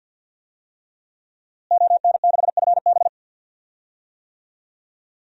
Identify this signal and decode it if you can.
Morse code, heard as a keyed tone, about 50 wpm, 700 Hz. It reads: ON6FB
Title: DXCC-100 Call Signs - 5 Characters - Repeated 3x - 50wpm